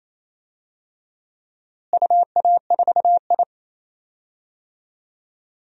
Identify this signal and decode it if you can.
Morse code, heard as a keyed tone, about 28 wpm, 700 Hz. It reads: UA4I
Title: DXCC-100 Call Signs - 4 Characters - Repeated 3x - 28wpm